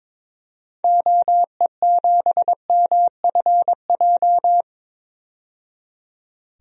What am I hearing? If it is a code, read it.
Morse code, heard as a keyed tone, about 22 wpm, 700 Hz. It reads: OE7MFJ
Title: DXCC-100 Call Signs - 6 Characters - 22wpm